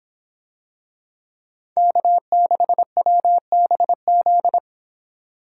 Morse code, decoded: K6WBZ

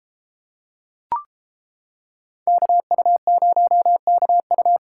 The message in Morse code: KU0KU

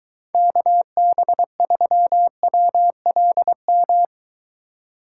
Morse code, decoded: KB3WLM

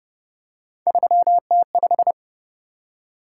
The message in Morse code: 3T5

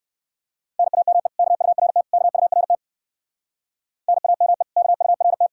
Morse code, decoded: NAME KURT KURT NAME KURT

